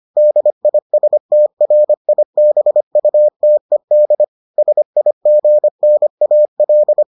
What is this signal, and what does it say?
Morse code, heard as a keyed tone, about 25 wpm, 600 Hz. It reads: DISTRIBUTED SIGNAL